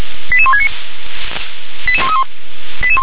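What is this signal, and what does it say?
Five-tone selective call (ZVEI), heard as quick runs of five tones, tones of about 70 ms. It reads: 88169 88131
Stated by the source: Five-tone ZVEI selective call received by SDR.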